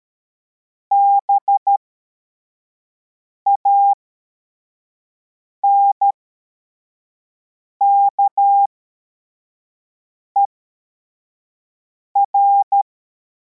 Morse code, heard as a keyed tone, about 13 wpm, spaced as for 4 wpm, 800 Hz. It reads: BANKER
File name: formation / cw / Text41